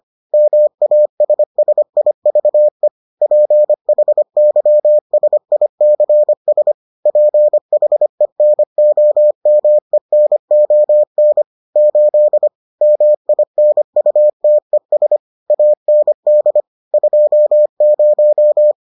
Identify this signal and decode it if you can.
Morse code, heard as a keyed tone, about 25 wpm, 600 Hz. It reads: MASSIVE PHYSICS PHENOMENON 8 MINUTES AND 20